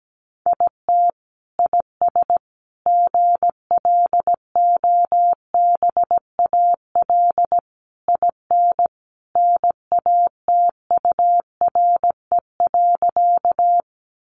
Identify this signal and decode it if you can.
Morse code, heard as a keyed tone, about 17 wpm, 700 Hz. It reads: IT IS GLOBAL IN NATURE.
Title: Sentences from Top 1200 Words 17wpm